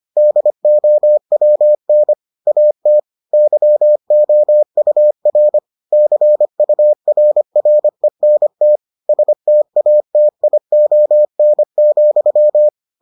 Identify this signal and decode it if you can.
Morse code, heard as a keyed tone, about 25 wpm, 600 Hz. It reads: DOWN AT YOUR CURRENT STATION,